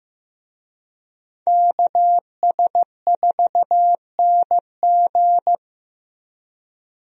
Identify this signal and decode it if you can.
Morse code, heard as a keyed tone, about 15 wpm, 700 Hz. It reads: KS4NG
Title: USA Call Signs - 5 Characters - Repeated 2x - 15wpm